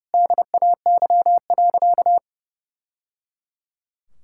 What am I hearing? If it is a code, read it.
Morse code, heard as a keyed tone, about 28 wpm, 700 Hz. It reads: DAY.
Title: Sentences from Top 1000 Words - Review Part 5 - 30wpm